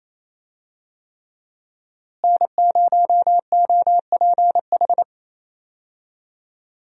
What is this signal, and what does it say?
Morse code, heard as a keyed tone, about 28 wpm, 700 Hz. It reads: N0OPH
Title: USA Call Signs - 5 Characters - Repeated 3x - 28wpm